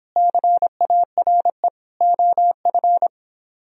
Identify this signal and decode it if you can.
Morse code, heard as a keyed tone, about 24 wpm, 700 Hz. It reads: CARE OF